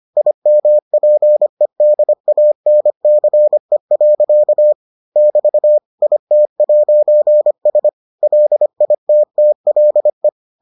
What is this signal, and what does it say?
Morse code, heard as a keyed tone, about 25 wpm, 600 Hz. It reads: IMPEDANCE. <BT> IT'S LITTLE